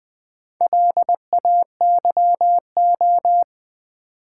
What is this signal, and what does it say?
Morse code, heard as a keyed tone, about 20 wpm, 690 Hz. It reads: LAYO
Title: Build A Word - 2 Letters At A Time - 6-Letter Words - 20wpm